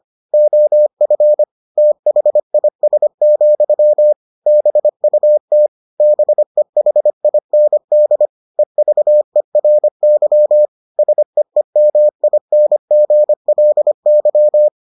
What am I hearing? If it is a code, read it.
Morse code, heard as a keyed tone, about 25 wpm, 600 Hz. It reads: OF THIS, BUT BEHIND EVERY SEEMINGLY